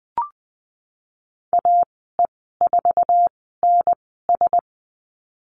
Morse code, heard as a keyed tone, about 20 wpm, spaced as for 15 wpm, 700 Hz. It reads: AE4NS
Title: 055 Lesson 10g - Letter r - Call Signs 20wpm